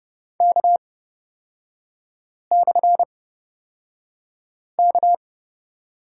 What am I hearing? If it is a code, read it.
Morse code, heard as a keyed tone, about 30 wpm, spaced as for 5 wpm, 700 Hz. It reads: K/K
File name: Lesson 36e - BT - Three Characters 30wpm